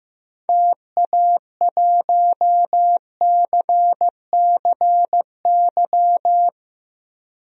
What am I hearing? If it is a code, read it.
Morse code, heard as a keyed tone, about 15 wpm, 700 Hz. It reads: TA1CCY
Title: DXCC-100 Call Signs - 6 Characters - Repeated 2x - 15wpm